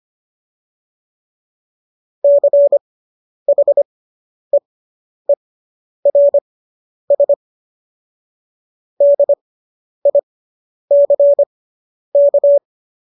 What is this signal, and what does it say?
Morse code, heard as a keyed tone, about 25 wpm, spaced as for 10 wpm, 570 Hz. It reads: CHEERS DICK